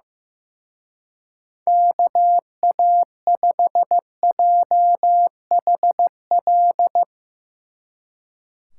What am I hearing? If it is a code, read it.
Morse code, heard as a keyed tone, about 16 wpm, 700 Hz. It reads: KA5JHL
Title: Speed Racing - USA 2x3 Call Signs 15wpm